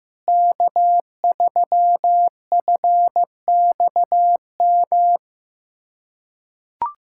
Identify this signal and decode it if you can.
Morse code, heard as a keyed tone, about 15 wpm, 700 Hz. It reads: K3FXM
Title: USA Callsign DE Callsign - 5 Characters - 15wpm